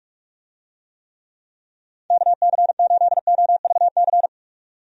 Morse code, heard as a keyed tone, about 46 wpm, 700 Hz. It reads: KC8OVC
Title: USA Call Signs - 6 Characters - 45wpm